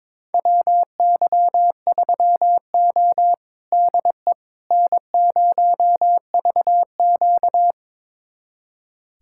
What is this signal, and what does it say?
Morse code, heard as a keyed tone, about 21 wpm, 700 Hz. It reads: WY3O DE N0VQ
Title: USA Callsign DE Callsign - 4 Characters - 22wpm